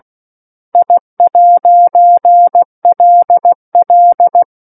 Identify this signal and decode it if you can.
Morse code, heard as a keyed tone, about 16 wpm, 700 Hz. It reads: I'LL
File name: 03-Chapter-03 16 WPM